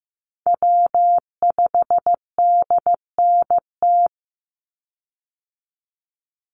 Morse code, heard as a keyed tone, about 15 wpm, 700 Hz. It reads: W5DNT
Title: Call Signs Repeated 3x - 15wpm